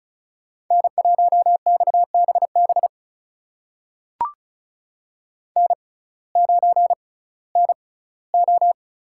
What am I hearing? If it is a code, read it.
Morse code, heard as a keyed tone, about 36 wpm, spaced as for 19 wpm, 700 Hz. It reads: N1XBB N9NO
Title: USA Call Signs - 5 Characters - 6x Character Spacing - 35wpm